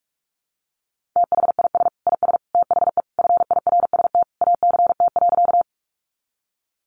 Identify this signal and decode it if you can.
Morse code, heard as a keyed tone, about 44 wpm, 700 Hz. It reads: THIS IS THE FIRST ACT.
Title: Sentences from Top 200 Words 45wpm